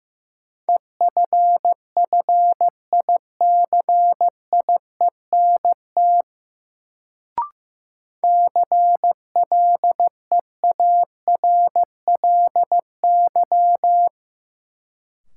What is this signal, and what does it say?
Morse code, heard as a keyed tone, about 16 wpm, 700 Hz. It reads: EFFICIENT CLEARLY
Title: Sentences from Top 1300 Words - New Word Drill Alt - 15wpm